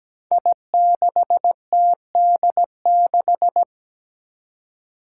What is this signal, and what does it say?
Morse code, heard as a keyed tone, about 17 wpm, 700 Hz. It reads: I6TD6
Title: Five Challenging Letter-Number Combinations 17wpm